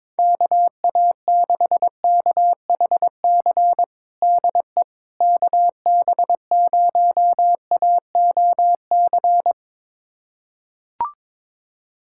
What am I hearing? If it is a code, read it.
Morse code, heard as a keyed tone, about 22 wpm, 700 Hz. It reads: KA6KHC DE KB0AOC
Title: USA Callsign DE Callsign - 6 Characters - 22wpm